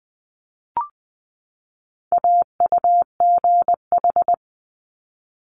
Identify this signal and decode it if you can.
Morse code, heard as a keyed tone, about 20 wpm, 700 Hz. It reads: AUGH
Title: Multi-letter Phonemes 20wpm